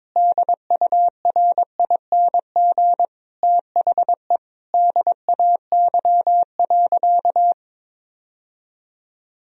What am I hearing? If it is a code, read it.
Morse code, heard as a keyed tone, about 22 wpm, 700 Hz. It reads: DURING THE DAY.
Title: Sentences from Top 1000 Words - Review Part 3 - 22wpm